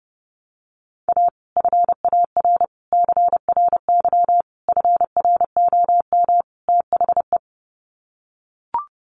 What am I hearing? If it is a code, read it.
Morse code, heard as a keyed tone, about 30 wpm, 700 Hz. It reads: A FAR CRY FROM THE
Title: Sets of 5 Words Encore 30wpm